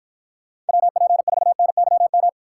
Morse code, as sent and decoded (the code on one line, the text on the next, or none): .-- .--. ....- -. -.-- --
WP4NYM